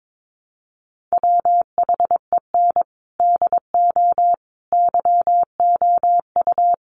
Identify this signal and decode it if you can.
Morse code, heard as a keyed tone, about 22 wpm, 700 Hz. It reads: WHEN DO YOU
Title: Sentences from Top 1100 Words 22wpm